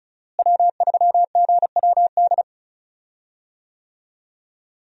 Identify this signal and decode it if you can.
Morse code, heard as a keyed tone, about 36 wpm, 700 Hz. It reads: W3GWD
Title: USA Call Signs - 5 Characters - Repeated 2x - 35wpm